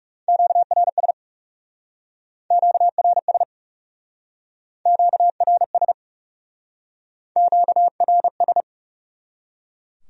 Morse code, heard as a keyed tone, about 37 wpm, 700 Hz. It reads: QRS QRS QRS QRS
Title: Speed-Racing Q-Codes - 30wpm